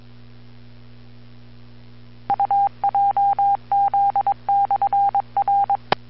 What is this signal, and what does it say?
Morse code, heard as a keyed tone, about 22 wpm, 800 Hz. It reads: UJZ/R
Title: Bluemont, VA Skywarn Repeater - WA4TSC